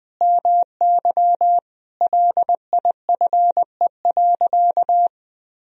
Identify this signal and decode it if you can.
Morse code, heard as a keyed tone, about 21 wpm, 700 Hz. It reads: MY LIFE.
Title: Sentences from Top 1000 Words - Review Part 2 - 20wpm